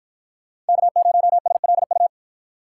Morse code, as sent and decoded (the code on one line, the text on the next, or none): -.- ----- ... .-.. ..-
K0SLU